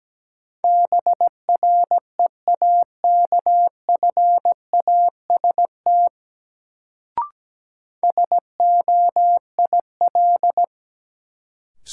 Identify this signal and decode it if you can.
Morse code, heard as a keyed tone, about 17 wpm, 700 Hz. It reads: BREAKFAST SOIL
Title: Sentences from Top 1500 Words - New Word Drill Alt - 17wpm